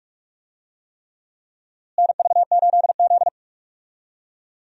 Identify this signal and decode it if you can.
Morse code, heard as a keyed tone, about 42 wpm, 700 Hz. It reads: NV8Z
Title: USA Call Signs - 4 Characters - Repeated 3x - 45wpm